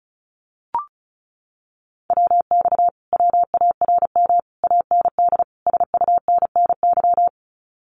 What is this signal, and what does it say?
Morse code, heard as a keyed tone, about 34 wpm, 700 Hz. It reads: WX WARM AND SUNNY
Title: Common QSO Phrases Encore 35wpm